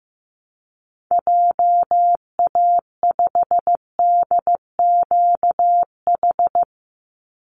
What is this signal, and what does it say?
Morse code, heard as a keyed tone, about 15 wpm, 700 Hz. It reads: JA5DQH
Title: Call Signs Repeated 3x - 15wpm